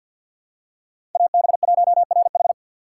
Morse code, heard as a keyed tone, about 47 wpm, 700 Hz. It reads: AB1RH
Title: USA Call Signs - 5 Characters - 4x Character Spacing - 50wpm